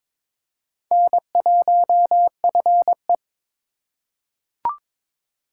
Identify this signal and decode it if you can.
Morse code, heard as a keyed tone, about 21 wpm, 700 Hz. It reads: N1FE
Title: K1USN SST QSO Practice - 22wpm